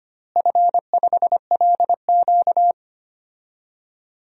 Morse code, heard as a keyed tone, about 25 wpm, 700 Hz. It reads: F5LQ
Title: DXCC-100 Call Signs - 4 Characters - Repeated 3x - 25wpm